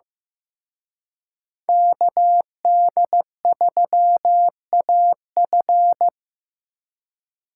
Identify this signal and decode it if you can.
Morse code, heard as a keyed tone, about 18 wpm, 700 Hz. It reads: KD3AF
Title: Speed Racing - USA 2x2 Call Signs 15wpm